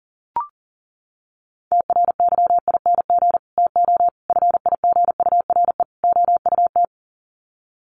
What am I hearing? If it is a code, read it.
Morse code, heard as a keyed tone, about 40 wpm, 700 Hz. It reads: TRYING TO FIGURE OUT
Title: Sets of 4 Words 40wpm